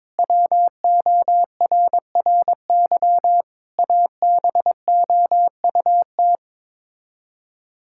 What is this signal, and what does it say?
Morse code, heard as a keyed tone, about 22 wpm, 700 Hz. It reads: WORRY ABOUT
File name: practice / 22wpm / Sets of 6 Words - 22wpm